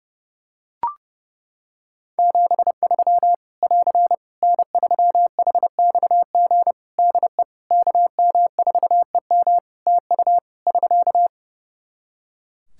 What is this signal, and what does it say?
Morse code, heard as a keyed tone, about 29 wpm, 700 Hz. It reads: 73 <AR> N3HXG DE KM4EM TU <SK>
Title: Practice QSO - Exchange 4 - Ending - 30wpm